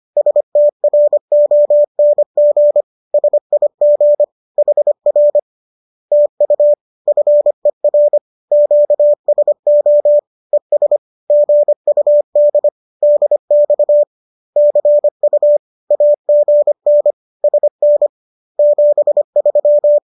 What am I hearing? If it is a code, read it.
Morse code, heard as a keyed tone, about 25 wpm, 590 Hz. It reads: STRONG SIG HR TU FER QSO ES GUD DX CU AGN SN 73